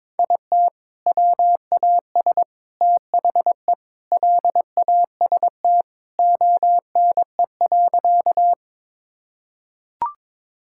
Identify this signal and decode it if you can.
Morse code, heard as a keyed tone, about 22 wpm, 700 Hz. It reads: IT WAS THE LAST ONE.
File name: practice / 22wpm / Sentences with 3-Word Phrases - 22wpm